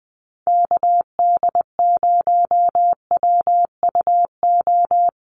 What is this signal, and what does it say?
Morse code, heard as a keyed tone, about 20 wpm, 700 Hz. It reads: KD0WUO
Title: USA Call Signs Repeated 3x 20wpm